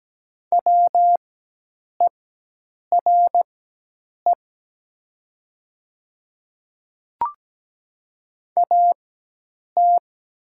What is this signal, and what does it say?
Morse code, heard as a keyed tone, about 17 wpm, spaced as for 8 wpm, 700 Hz. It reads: WERE AT
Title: 4x Character Spacing - Top 100 Words - 17wpm